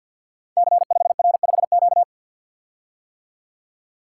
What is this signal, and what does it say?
Morse code, heard as a keyed tone, about 48 wpm, 700 Hz. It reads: C5R5Q
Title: Code Groups - 5 Letter:Numbers - 50wpm